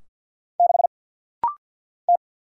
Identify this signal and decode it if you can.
Morse code, heard as a keyed tone, about 50 wpm, 700 Hz. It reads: 6 T
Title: Single Letter-Number Alt - Rapid-Fire 50wpm